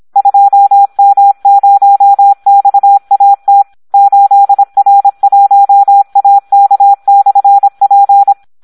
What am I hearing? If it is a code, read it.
Morse code, heard as a keyed tone, about 26 wpm, 800 Hz, with radio noise. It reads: 2M0XAT 8R1AK/P
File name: Audio practica CW / SD 1 26